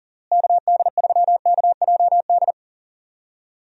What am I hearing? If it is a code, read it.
Morse code, heard as a keyed tone, about 40 wpm, 700 Hz. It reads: KD3KJD
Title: USA Call Signs - 6 Characters - Repeated 2x - 40wpm